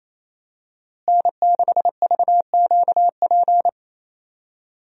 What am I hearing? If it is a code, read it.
Morse code, heard as a keyed tone, about 28 wpm, 700 Hz. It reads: N6VQP
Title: USA Call Signs - 5 Characters - Repeated 3x - 28wpm